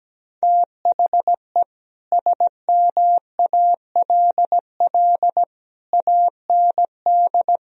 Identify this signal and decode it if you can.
Morse code, heard as a keyed tone, about 17 wpm, 700 Hz. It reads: THE SMALL AND
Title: Sentences from Top 1000 Words - Review Part 1 - 17wpm